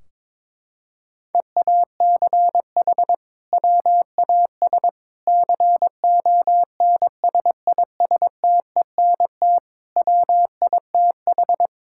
Morse code, heard as a keyed tone, about 22 wpm, 700 Hz. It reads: EACH WAS CONSISTENT WITH
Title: Sentences from Top 1600 Words 22wpm